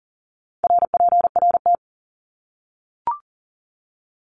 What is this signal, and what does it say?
Morse code, heard as a keyed tone, about 40 wpm, 700 Hz. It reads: RPRT
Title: Abbreviations 40wpm